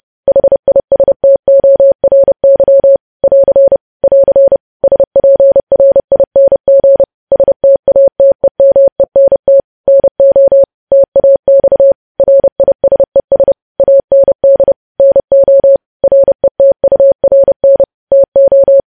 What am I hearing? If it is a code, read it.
Morse code, heard as a keyed tone, about 30 wpm, 550 Hz. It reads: HISTORY <AR> <AR> SPRING STATEMENT NO TAX RISES AND NO RETURN TO
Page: news headlines in morse code at 30 wpm